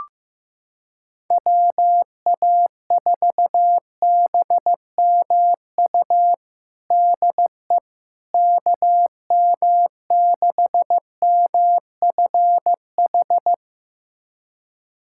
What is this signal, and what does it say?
Morse code, heard as a keyed tone, about 15 wpm, 700 Hz. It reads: WA4BMU DE KM6MFH
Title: USA Callsign DE Callsign 15wpm